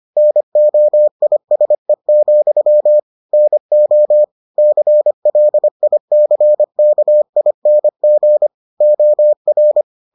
Morse code, heard as a keyed tone, about 25 wpm, 600 Hz. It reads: NOISE, NO CLICKING OR